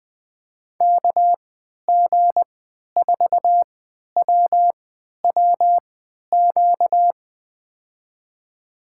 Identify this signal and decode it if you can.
Morse code, heard as a keyed tone, about 20 wpm, spaced as for 11 wpm, 700 Hz. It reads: KG4WWQ
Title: USA Call Signs - 6 Characters - 3x Character Spacing - 20wpm